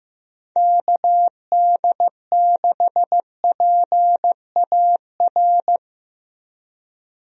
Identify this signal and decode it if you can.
Morse code, heard as a keyed tone, about 15 wpm, 700 Hz. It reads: KD6PAR
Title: USA Call Signs - 6 Characters - Repeated 3x - 15wpm